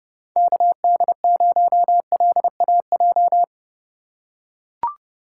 Morse code, heard as a keyed tone, about 30 wpm, 700 Hz. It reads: KD0LAJ
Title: USA Call Signs - 6 Characters - 3x Character Spacing - 30wpm